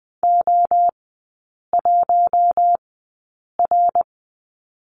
Morse code, heard as a keyed tone, about 20 wpm, spaced as for 8 wpm, 700 Hz. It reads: O1R
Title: 060 Lesson 11e - Letter h - Three Characters 20wpm